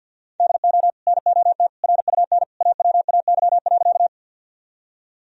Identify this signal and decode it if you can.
Morse code, heard as a keyed tone, about 52 wpm, 700 Hz. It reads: DO NOT RUN AWAY.
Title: Sentences from Top 1000 Words - Review Part 1 - 50wpm